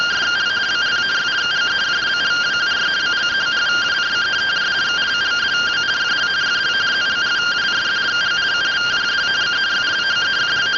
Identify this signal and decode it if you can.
Two-tone teleprinter signal, about 170 Hz shift.
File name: findings / CT 205 ax mips, iDose 3 0007.gen-25 rtty rx